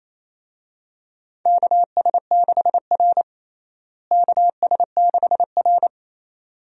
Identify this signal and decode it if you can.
Morse code, heard as a keyed tone, about 28 wpm, 700 Hz. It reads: KS6R KS6R
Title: USA Call Signs - 4 Characters - Repeated 2x - 28wpm